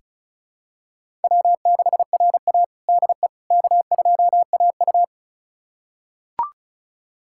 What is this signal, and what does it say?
Morse code, heard as a keyed tone, about 36 wpm, 700 Hz. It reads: W6RA DE K2AU
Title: USA Callsign DE Callsign - 4 Characters - 35wpm